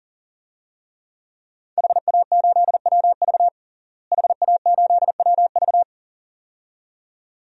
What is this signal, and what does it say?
Morse code, heard as a keyed tone, about 40 wpm, 700 Hz. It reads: HA8WV HA8WV
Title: DXCC-100 Call Signs - 5 Characters - Repeated 2x - 40wpm